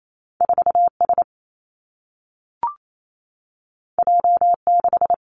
Morse code, heard as a keyed tone, about 28 wpm, 700 Hz. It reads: 4 S J 6
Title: Two Challenging Letter-Number Combinations 28wpm